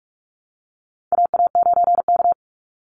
Morse code, heard as a keyed tone, about 44 wpm, 700 Hz. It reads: AA9K
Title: Call Signs Repeated 3x - 45wpm